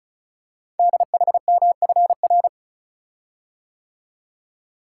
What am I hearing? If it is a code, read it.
Morse code, heard as a keyed tone, about 36 wpm, 700 Hz. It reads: DHMFR
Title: Code Groups - 5 Letters - 35wpm